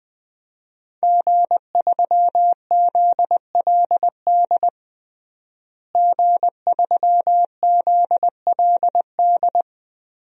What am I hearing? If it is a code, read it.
Morse code, heard as a keyed tone, about 20 wpm, 700 Hz. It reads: G3ZLD G3ZLD
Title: DXCC-100 Call Signs - 5 Characters - Repeated 3x - 20wpm